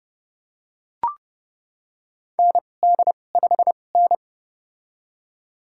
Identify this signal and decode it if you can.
Morse code, heard as a keyed tone, about 30 wpm, spaced as for 22 wpm, 700 Hz. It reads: ND5N
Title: USA Call Signs - 2x Character Spacing - 30wpm